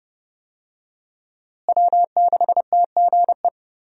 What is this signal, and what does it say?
Morse code, heard as a keyed tone, about 31 wpm, 700 Hz. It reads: W6TGE
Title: USA Call Signs - 5 Characters - Repeated 3x - 30wpm